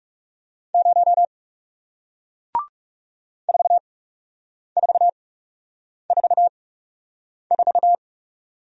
Morse code, heard as a keyed tone, about 40 wpm, spaced as for 8 wpm, 700 Hz. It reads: 0 4444